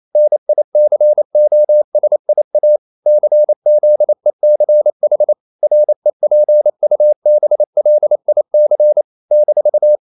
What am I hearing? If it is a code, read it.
Morse code, heard as a keyed tone, about 28 wpm, 600 Hz. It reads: NICOSIA CZECH REPUBLIC -
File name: CW / Practice / NationalCapitals28